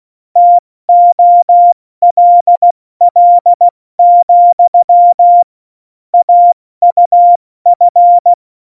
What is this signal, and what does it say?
Morse code, heard as a keyed tone, about 16 wpm, 700 Hz. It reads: TOLL, AUF
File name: Morsekurs / Witze / Witz 21. November 2017, 70 F80